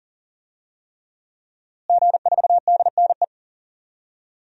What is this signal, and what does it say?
Morse code, heard as a keyed tone, about 40 wpm, 700 Hz. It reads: G4DNE
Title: DXCC-100 Call Signs - 5 Characters - Repeated 3x - 40wpm